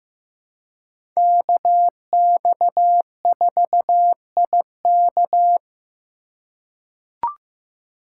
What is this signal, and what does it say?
Morse code, heard as a keyed tone, about 15 wpm, 700 Hz. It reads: KX4IK
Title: USA Call Signs - 4x Character Spacing - 15wpm